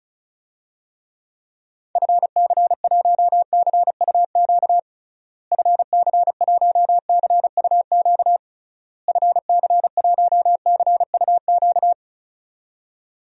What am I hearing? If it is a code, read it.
Morse code, heard as a keyed tone, about 36 wpm, 700 Hz. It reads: FC1CUQ FC1CUQ FC1CUQ
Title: DXCC-100 Call Signs - 6 Characters - Repeated 3x - 35wpm